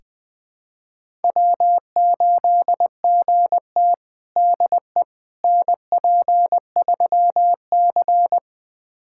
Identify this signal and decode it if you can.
Morse code, heard as a keyed tone, about 20 wpm, 700 Hz. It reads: W8GT DE NP3C
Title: USA Callsign DE Callsign - 4 Characters - 20wpm